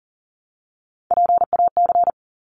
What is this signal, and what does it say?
Morse code, heard as a keyed tone, about 40 wpm, 700 Hz. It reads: PAC
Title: ARRL Sections 40wpm